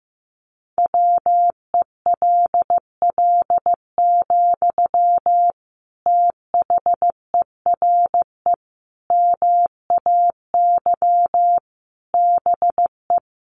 Morse code, heard as a keyed tone, about 15 wpm, 700 Hz. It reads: WELL, THERE MAY BE
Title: Sentences from Top 1300 Words 15wpm